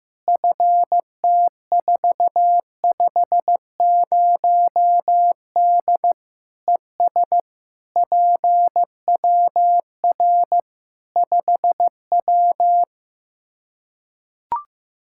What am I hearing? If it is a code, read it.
Morse code, heard as a keyed tone, about 15 wpm, 700 Hz. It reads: FT450D ES PWR 5W
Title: Practice QSO - Exchange 2 - Info - 15wpm